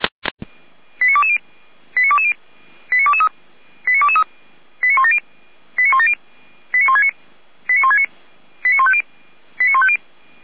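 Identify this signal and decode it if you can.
Five-tone selective call (ZVEI), heard as quick runs of five tones, tones of about 70 ms. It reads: 89220 89220 89223 89223 89170 89170 89169 89169 89150 89150
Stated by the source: Five-tone ZVEI selective call received by SDR.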